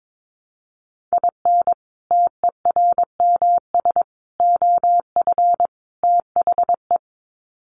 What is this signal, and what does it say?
Morse code, heard as a keyed tone, about 22 wpm, 700 Hz. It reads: IN TERMS OF THE
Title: Sets of 4 Words 22wpm